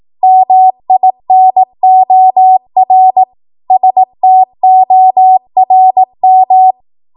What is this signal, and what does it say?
Morse code, heard as a keyed tone, about 18 wpm, 750 Hz. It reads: MINOR STORM
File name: Morse / 18WPM